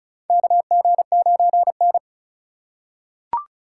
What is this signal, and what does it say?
Morse code, heard as a keyed tone, about 36 wpm, 700 Hz. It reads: KG9N
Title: USA Call Signs - 3x Character Spacing - 35wpm